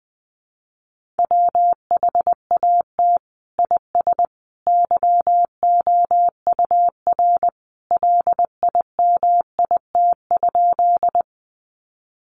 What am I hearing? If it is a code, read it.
Morse code, heard as a keyed tone, about 20 wpm, 700 Hz. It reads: WHAT IS YOUR LIMIT?
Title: Sentences from Top 1500 Words 20wpm